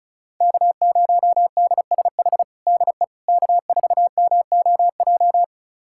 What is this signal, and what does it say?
Morse code, heard as a keyed tone, about 34 wpm, 700 Hz. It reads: K0DSH DE K4MOJ
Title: USA Callsign DE Callsign - 5 Characters - 35wpm